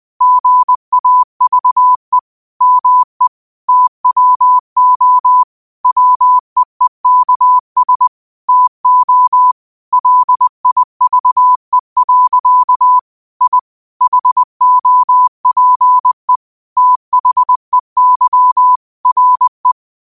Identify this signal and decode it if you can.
Morse code, heard as a keyed tone, about 20 wpm, 1000 Hz. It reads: GAVE ME TWO WEEKS TO LIVE. I HOPE THEY RE
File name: qotdfiles / 20wpm